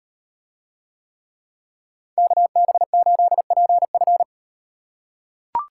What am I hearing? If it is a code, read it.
Morse code, heard as a keyed tone, about 38 wpm, 700 Hz. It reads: KB8PF